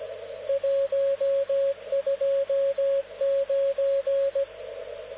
Morse code, heard as a keyed tone, about 17 wpm, 550 Hz, with radio noise. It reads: 129
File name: luxorion / Radio / dk0wcy-spaceweather-10144khz